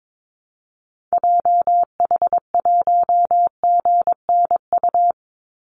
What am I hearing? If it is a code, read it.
Morse code, heard as a keyed tone, about 22 wpm, 690 Hz. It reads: JH1GNU
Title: Call Signs Repeated 2x - 22wpm